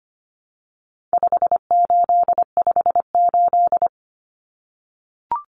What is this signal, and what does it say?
Morse code, heard as a keyed tone, about 25 wpm, 700 Hz. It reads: 5858
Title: Numbers - 4-Digit 25wpm